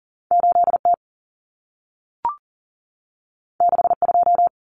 Morse code, heard as a keyed tone, about 40 wpm, 700 Hz. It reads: 8T 62
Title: Common Cut Numbers - 2-Digit 40wpm